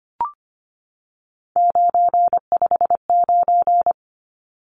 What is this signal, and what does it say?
Morse code, heard as a keyed tone, about 25 wpm, 700 Hz. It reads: 959
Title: Numbers - 3-Digit 25wpm